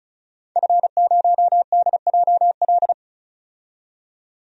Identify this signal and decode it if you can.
Morse code, heard as a keyed tone, about 36 wpm, 700 Hz. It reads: F0DJL
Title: DXCC-100 Call Signs - 5 Characters - 35wpm